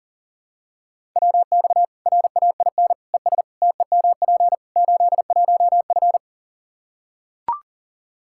Morse code, heard as a keyed tone, about 40 wpm, 700 Hz. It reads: WX RAIN ES TEMP 81F
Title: Practice QSO - Exchange 2 - Info - 40wpm